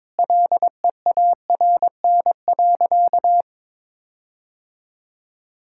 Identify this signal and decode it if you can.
Morse code, heard as a keyed tone, about 22 wpm, 700 Hz. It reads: LEARN.
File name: Difficult Sentences with Hints - 22wpm